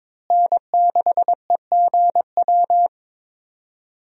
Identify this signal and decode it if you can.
Morse code, heard as a keyed tone, about 22 wpm, 700 Hz. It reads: N6EGW
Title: USA Call Signs - 5 Characters - Repeated 3x - 22wpm